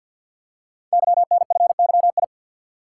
Morse code, heard as a keyed tone, about 48 wpm, 700 Hz. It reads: YNFYI
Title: Code Groups - 5 Letters - 50wpm